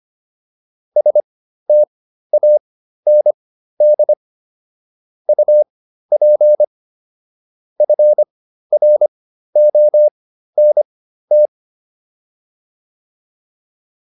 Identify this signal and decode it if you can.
Morse code, heard as a keyed tone, about 25 wpm, spaced as for 13 wpm, 600 Hz. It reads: STAND UP FRONT
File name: PR405 13